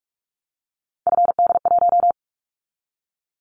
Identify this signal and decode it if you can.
Morse code, heard as a keyed tone, about 44 wpm, 700 Hz. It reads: FD1
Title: Callsign Prefixes for Top 50 DXCC - 45wpm